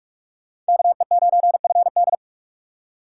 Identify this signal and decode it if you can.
Morse code, heard as a keyed tone, about 61 wpm, spaced as for 52 wpm, 700 Hz. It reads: KE9FD
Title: Speed Racing - USA 2x2 Call Signs 45wpm